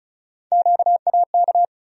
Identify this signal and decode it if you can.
Morse code, heard as a keyed tone, about 34 wpm, 700 Hz. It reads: QAK
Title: USA Call Sign Postfixes - 3 Letters 35wpm